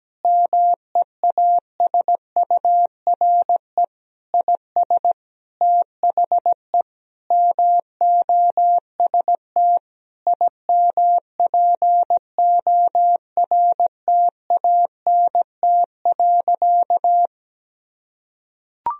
Morse code, heard as a keyed tone, about 17 wpm, 700 Hz. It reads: MEASURE IS THE MOST IMPORTANT.
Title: Sentences from Top 1000 Words - Review Part 2 - 17wpm